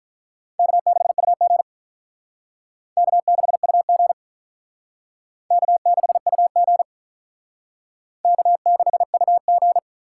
Morse code, heard as a keyed tone, about 47 wpm, 700 Hz. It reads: K6UG K6UG K6UG K6UG